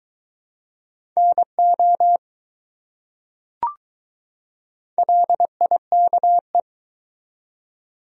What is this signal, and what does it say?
Morse code, heard as a keyed tone, about 21 wpm, 700 Hz. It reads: NO LIKE